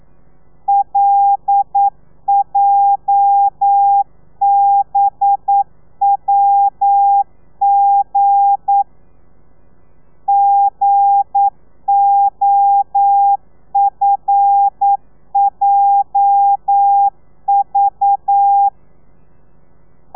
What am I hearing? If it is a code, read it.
Morse code, heard as a keyed tone, about 9 wpm, 800 Hz. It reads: LJBWG GOFJV
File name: telegrafia / i2tfj / Lez8 40